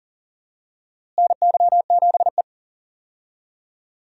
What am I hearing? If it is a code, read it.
Morse code, heard as a keyed tone, about 40 wpm, 700 Hz. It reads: NY7E
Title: USA Call Signs - 4 Characters - Repeated 2x - 40wpm